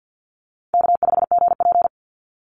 Morse code, heard as a keyed tone, about 50 wpm, 700 Hz. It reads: K5GP
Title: Easy Call Signs Repeated 2x - 50wpm